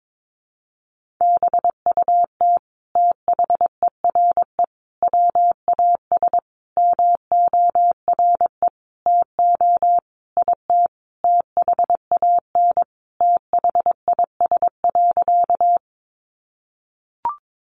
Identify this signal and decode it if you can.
Morse code, heard as a keyed tone, about 22 wpm, 700 Hz. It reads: BUT THERE WAS MORE TO IT THAN THIS.
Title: Sentences from Top 100 Words 22wpm